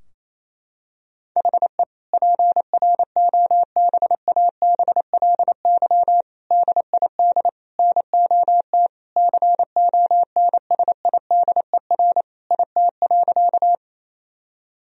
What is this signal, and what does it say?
Morse code, heard as a keyed tone, about 28 wpm, 700 Hz. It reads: HE PROBABLY DID NOT CONSIDER IT.
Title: Sentences from Top 1000 Words - Review Part 4 - 28wpm